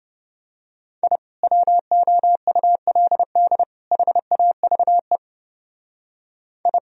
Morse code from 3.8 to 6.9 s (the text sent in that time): HAVE I